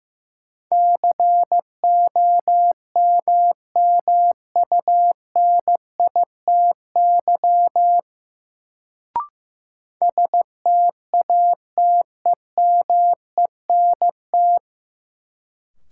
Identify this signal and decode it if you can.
Morse code, heard as a keyed tone, about 15 wpm, 700 Hz. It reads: COMMUNITY STATEMENT
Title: Sentences from Top 1200 Words - New Word Drill Alt - 15wpm